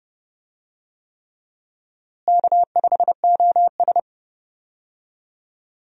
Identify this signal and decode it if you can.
Morse code, heard as a keyed tone, about 30 wpm, 700 Hz. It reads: K5OS
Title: USA Call Signs - 4 Characters - Repeated 3x - 30wpm